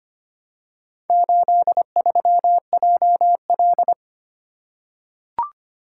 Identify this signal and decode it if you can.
Morse code, heard as a keyed tone, about 25 wpm, 700 Hz. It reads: 83JL
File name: practice / 25wpm / Four Letter-Number Combinations - 25wpm